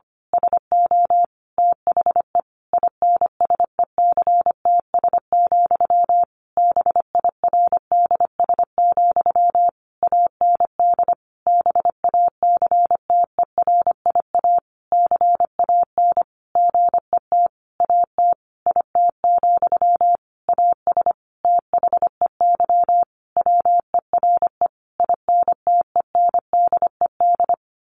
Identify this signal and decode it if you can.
Morse code, heard as a keyed tone, about 25 wpm, 700 Hz. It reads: SO THE INSECTS, BIRDS, AND BACTERIA CAN GET AT IT, AS THEY WERE INTENDED
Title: A Sound Of Thunder 25wpm